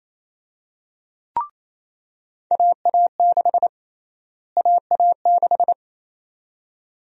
Morse code, heard as a keyed tone, about 28 wpm, 700 Hz. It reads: AA6 AA6
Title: USA Call Sign Prefixes - Repeated 2x - 28wpm